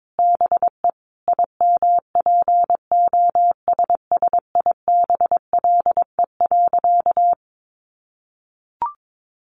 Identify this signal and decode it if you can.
Morse code, heard as a keyed tone, about 22 wpm, 700 Hz. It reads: BE IMPOSSIBLE.
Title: Sentences from Top 1100 Words 22wpm